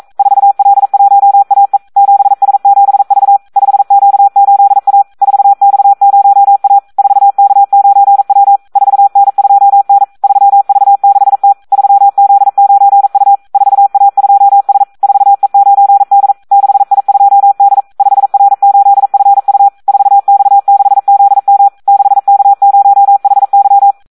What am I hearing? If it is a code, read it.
Morse code, heard as a keyed tone, about 42 wpm, 800 Hz. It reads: 4L1AE 7S7V 5Q8A 4X0A 4K9W 4N2N 3V6T 3Z9U 4A2S 4E9D 6I2D 5R8FU 4X6ZM 6K0HY
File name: Audio practica CW / SD 1 42